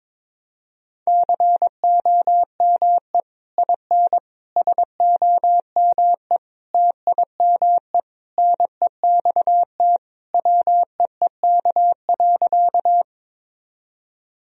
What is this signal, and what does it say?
Morse code, heard as a keyed tone, about 21 wpm, 700 Hz. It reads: COME IN SOME TIME NEXT WEEK.
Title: Sentences from Top 1000 Words - Review Part 2 - 22wpm